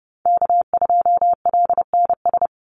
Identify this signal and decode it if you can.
Morse code, heard as a keyed tone, about 30 wpm, 700 Hz. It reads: K2LNS
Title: Call Signs Repeated 3x - 30wpm